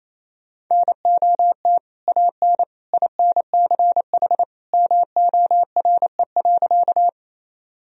Morse code, heard as a keyed tone, about 28 wpm, 700 Hz. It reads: NOT AN INCH MORE.
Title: Sentences from Top 1000 Words - Review Part 3 - 28wpm